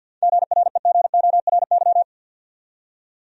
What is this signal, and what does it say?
Morse code, heard as a keyed tone, about 50 wpm, 700 Hz. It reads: GREGORY
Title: USA Common Names - 50wpm